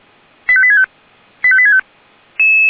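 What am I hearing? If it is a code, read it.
Five-tone selective call (ZVEI), heard as quick runs of five tones, tones of about 70 ms. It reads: 85865 85865
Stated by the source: SDR-received ZVEI five-tone call, code 85866.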